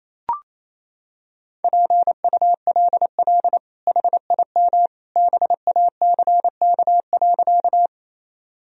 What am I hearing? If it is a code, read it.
Morse code, heard as a keyed tone, about 28 wpm, 700 Hz. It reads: PULL HIM BACK.
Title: Sentences from Top 1000 Words - Review Part 2 - 28wpm